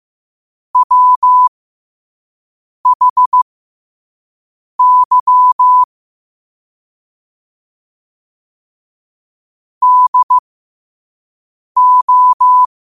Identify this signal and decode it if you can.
Morse code, heard as a keyed tone, about 15 wpm, spaced as for 5 wpm, 1000 Hz. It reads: WHY DO
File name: qotdfiles / 5wpm